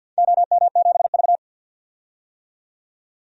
Morse code, heard as a keyed tone, about 50 wpm, 700 Hz. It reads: OM7V
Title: DXCC-100 Call Signs - 4 Characters - 50wpm